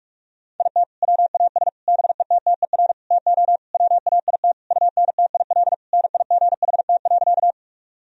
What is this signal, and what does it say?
Morse code, heard as a keyed tone, about 46 wpm, 700 Hz. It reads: IT WAS BETTER TO WAIT UNTIL NIGHT.
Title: Sentences from Top 1000 Words - Review Part 1 - 45wpm